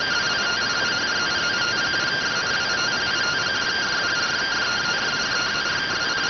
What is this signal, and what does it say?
Two-tone teleprinter signal, about 170 Hz shift.